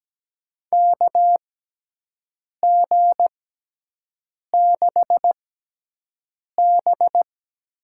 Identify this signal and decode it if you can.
Morse code, heard as a keyed tone, about 17 wpm, spaced as for 6 wpm, 700 Hz. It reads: KG6B
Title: USA Call Signs - 6 Characters - 6x Character Spacing - 17wpm